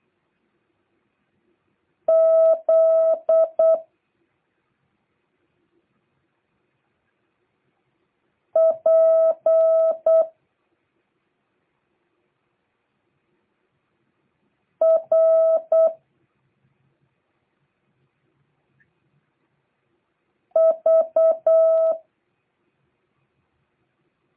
Morse code, heard as a keyed tone, about 8 wpm, 650 Hz. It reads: Z P R V